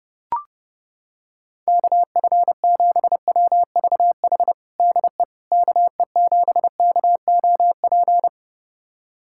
Morse code, heard as a keyed tone, about 30 wpm, 700 Hz. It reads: KF7WVH DE KE7KOP
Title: USA Callsign DE Callsign - 6 Characters - 30wpm